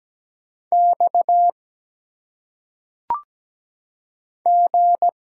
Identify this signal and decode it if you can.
Morse code, heard as a keyed tone, about 17 wpm, 700 Hz. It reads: X G
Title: Single Letters - Phonetic Pronunciation 17wpm